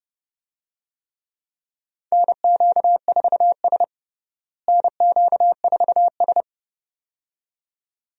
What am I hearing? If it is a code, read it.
Morse code, heard as a keyed tone, about 30 wpm, 700 Hz. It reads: NQ4S NQ4S
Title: USA Call Signs - 4 Characters - Repeated 2x - 30wpm